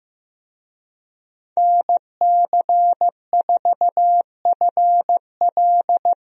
Morse code, heard as a keyed tone, about 15 wpm, 700 Hz. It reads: NC4FL
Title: USA Call Signs - 5 Characters - Repeated 2x - 15wpm